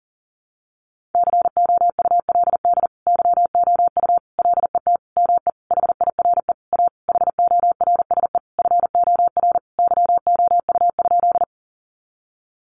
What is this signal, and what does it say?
Morse code, heard as a keyed tone, about 40 wpm, 700 Hz. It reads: COULD YOU LET ME HIRE A HORSE FOR YOU?
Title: Sentences from Top 1400 Words 40wpm